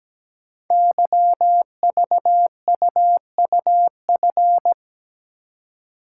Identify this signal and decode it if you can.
Morse code, heard as a keyed tone, about 17 wpm, 700 Hz. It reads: YVUUF
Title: Code Groups - 5 Letter:Numbers - 17wpm